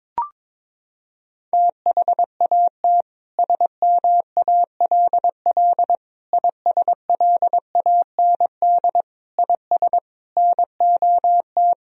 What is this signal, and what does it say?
Morse code, heard as a keyed tone, about 22 wpm, 700 Hz. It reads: THAT SMALL ISLAND IS NOT
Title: Sentences from Top 1000 Words - Review Part 4 - 22wpm